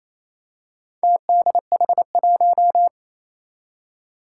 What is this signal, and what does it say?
Morse code, heard as a keyed tone, about 28 wpm, 700 Hz. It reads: TDH1
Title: Four Challenging Letter-Number Combinations 28wpm